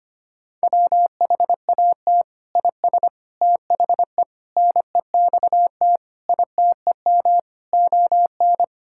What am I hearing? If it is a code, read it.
Morse code, heard as a keyed tone, about 25 wpm, 700 Hz. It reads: WHAT IS THE NEXT ITEM ON
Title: Sentences from Top 1000 Words - Review Part 6 - 25wpm